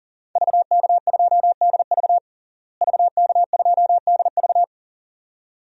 Morse code, heard as a keyed tone, about 37 wpm, 700 Hz. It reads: VK2DV VK2DV
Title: DXCC-100 Call Signs - 5 Characters - Repeated 2x - 40wpm